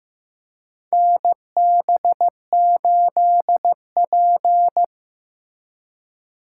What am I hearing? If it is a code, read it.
Morse code, heard as a keyed tone, about 15 wpm, 700 Hz. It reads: NB8P
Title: USA Call Signs - 4 Characters - 3x Character Spacing - 15wpm